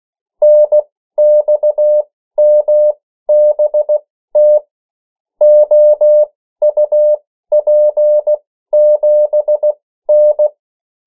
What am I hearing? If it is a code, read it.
Morse code, heard as a keyed tone, about 16 wpm, spaced as for 13 wpm, 600 Hz. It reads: NXMBT OUP7N
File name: morse mp3 / 14 urok 13